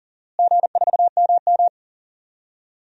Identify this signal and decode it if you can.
Morse code, heard as a keyed tone, about 40 wpm, 700 Hz. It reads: G4MM
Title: DXCC-100 Call Signs - 4 Characters - Repeated 2x - 40wpm